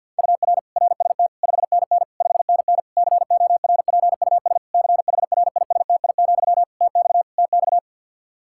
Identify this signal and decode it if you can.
Morse code, heard as a keyed tone, about 48 wpm, 700 Hz. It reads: UR RST 5NN 5NN CORPUS CHRISTI, TX TX
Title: K1USN SST QSO Practice 50wpm